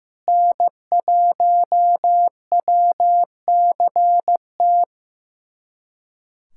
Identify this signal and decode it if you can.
Morse code, heard as a keyed tone, about 16 wpm, 700 Hz. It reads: N1WCT
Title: USA Callsign DE Callsign - 5 Characters - 15wpm